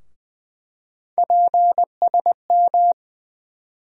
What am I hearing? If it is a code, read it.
Morse code, heard as a keyed tone, about 20 wpm, 700 Hz. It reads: PSM
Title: USA Call Sign Postfixes - 3 Letters 20wpm